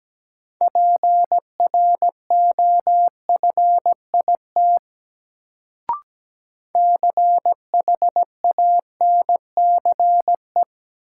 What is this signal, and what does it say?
Morse code, heard as a keyed tone, about 17 wpm, 700 Hz. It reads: PROFIT CHANCE
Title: Sentences from Top 800 Words - New Word Drill - 17wpm